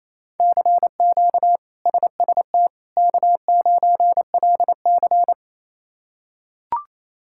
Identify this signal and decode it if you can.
Morse code, heard as a keyed tone, about 28 wpm, 700 Hz. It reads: CQ SST K9LC
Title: K1USN SST QSO Practice - 28wpm